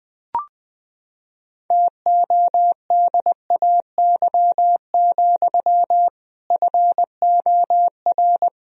Morse code, heard as a keyed tone, about 21 wpm, 700 Hz. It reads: TODAY, FOR
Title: Sentences from Top 1000 Words - Review Part 1 - 20wpm